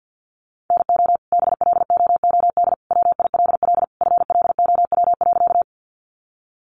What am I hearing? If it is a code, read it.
Morse code, heard as a keyed tone, about 49 wpm, 700 Hz. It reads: NO BLOOD WILL FLOW.
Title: Sentences from Top 1200 Words 50wpm